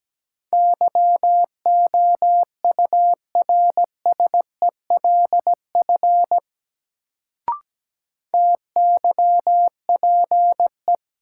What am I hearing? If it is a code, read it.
Morse code, heard as a keyed tone, about 17 wpm, 700 Hz. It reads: YOURSELF TYPE
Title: Sentences from Top 600 Words - New Word Drill - 17wpm